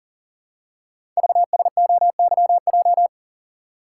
Morse code, heard as a keyed tone, about 42 wpm, 700 Hz. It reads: VSOYJ
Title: Code Groups - 5 Letters - 40wpm